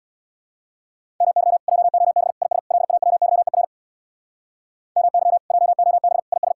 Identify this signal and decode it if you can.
Morse code, heard as a keyed tone, about 65 wpm, 700 Hz. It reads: MY JOB IS LAWYER MY JOB IS